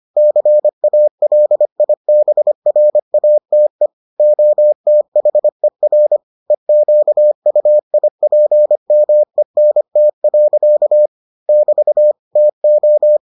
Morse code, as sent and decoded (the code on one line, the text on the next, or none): -.-. .- .-.. .. -... .-. .- - . / --- - .... .-. / . --.- ..- .. .--. -- . -. - .-.-.- / -...- / - ---
CALIBRATE OTHER EQUIPMENT. <BT> TO